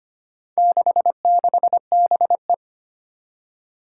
Morse code, heard as a keyed tone, about 25 wpm, 700 Hz. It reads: 66BE
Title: Four Challenging Letter-Number Combinations 25wpm